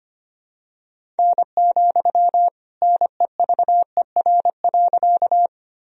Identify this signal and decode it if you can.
Morse code, heard as a keyed tone, about 25 wpm, 700 Hz. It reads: N, NEVER.
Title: Sentences from Top 300 Words and Abbreviations 25wpm